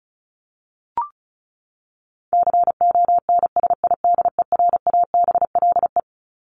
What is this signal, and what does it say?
Morse code, heard as a keyed tone, about 34 wpm, 700 Hz. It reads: CONSIDERABLE
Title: LY Suffix - Top 100 Words with Base Words 35wpm